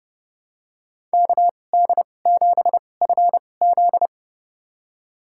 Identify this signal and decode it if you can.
Morse code, heard as a keyed tone, about 32 wpm, spaced as for 22 wpm, 690 Hz. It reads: KD7FZ
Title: USA Call Signs - 5 Characters - 2x Character Spacing - 30wpm